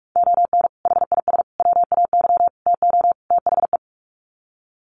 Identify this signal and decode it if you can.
Morse code, heard as a keyed tone, about 44 wpm, 700 Hz. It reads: ON HIS WAY TO THE
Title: Sets of 5 Words Encore2 45wpm